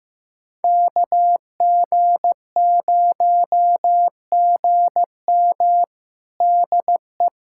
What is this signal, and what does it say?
Morse code, heard as a keyed tone, about 15 wpm, 700 Hz. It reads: KG0GM DE
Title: USA Callsign DE Callsign - 5 Characters - 15wpm